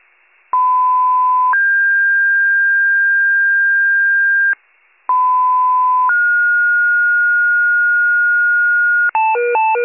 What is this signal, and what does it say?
Two-tone page, heard as a one-second tone, then a three-second tone, about 1010 then 1600 Hz, about 1010 then 1430 Hz, with noise. It is heard from 0.5 to 9.1 s.